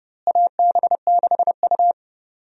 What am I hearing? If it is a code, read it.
Morse code, heard as a keyed tone, about 30 wpm, 700 Hz. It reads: AB6U
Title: USA Call Signs - 4 Characters - Repeated 3x - 30wpm